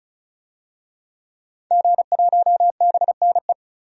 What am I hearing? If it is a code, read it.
Morse code, heard as a keyed tone, about 34 wpm, 700 Hz. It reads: G1BNE
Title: DXCC-100 Call Signs - 5 Characters - 35wpm